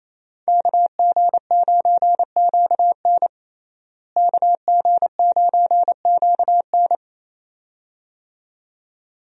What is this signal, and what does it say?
Morse code, heard as a keyed tone, about 28 wpm, 700 Hz. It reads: KG9QN KG9QN
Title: USA Call Signs - 5 Characters - Repeated 2x - 28wpm